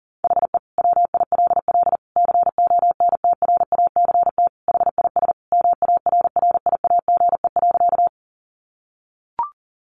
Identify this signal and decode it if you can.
Morse code, heard as a keyed tone, about 40 wpm, 700 Hz. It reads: HE WILL CONTRACT HIS MARRIAGE.
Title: Sentences from Top 1400 Words 40wpm